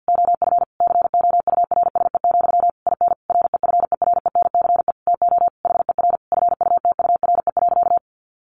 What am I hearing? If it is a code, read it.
Morse code, heard as a keyed tone, about 50 wpm, 700 Hz. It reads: OF COURSE, IN REFERENCE TO HER FUTURE.
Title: Sentences from Top 1200 Words 50wpm